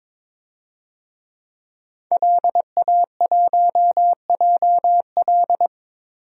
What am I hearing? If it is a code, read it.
Morse code, heard as a keyed tone, about 21 wpm, 700 Hz. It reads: LA1JL
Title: DXCC-100 Call Signs - 5 Characters - Repeated 2x - 22wpm